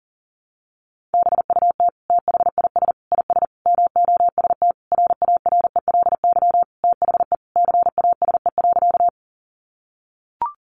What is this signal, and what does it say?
Morse code, heard as a keyed tone, about 40 wpm, 700 Hz. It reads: BUT THIS IS MOST RARELY THE CASE.
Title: Sentences from Top 1400 Words 40wpm